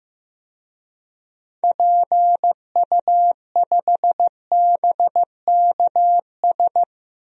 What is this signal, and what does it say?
Morse code, heard as a keyed tone, about 15 wpm, 700 Hz. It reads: PU5BKS
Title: DXCC-100 Call Signs - 6 Characters - Repeated 3x - 15wpm